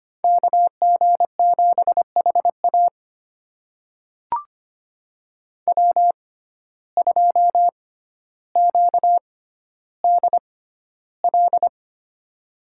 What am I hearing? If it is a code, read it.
Morse code, heard as a keyed tone, about 25 wpm, spaced as for 13 wpm, 700 Hz. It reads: KG7HA W2QDL
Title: USA Call Signs - 5 Characters - 6x Character Spacing - 25wpm